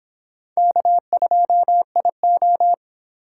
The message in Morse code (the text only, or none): K2IO